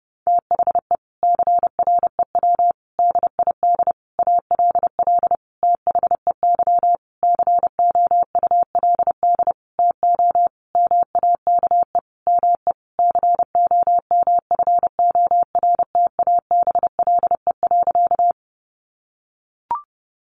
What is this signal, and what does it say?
Morse code, heard as a keyed tone, about 30 wpm, 700 Hz. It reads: THE CREW DID ALL THEY COULD TO MAKE ME COMFORTABLE.
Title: Sentences from Top 1500 Words 30wpm